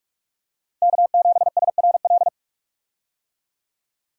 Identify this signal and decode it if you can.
Morse code, heard as a keyed tone, about 45 wpm, 700 Hz. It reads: K7SRL
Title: USA Call Signs - 5 Characters - 45wpm